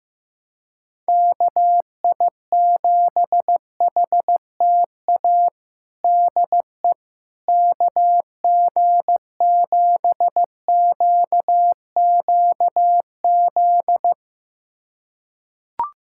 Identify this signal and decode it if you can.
Morse code, heard as a keyed tone, about 15 wpm, 700 Hz. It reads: KI7HTA DE KG7QQZ
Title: USA Callsign DE Callsign - 6 Characters - 15wpm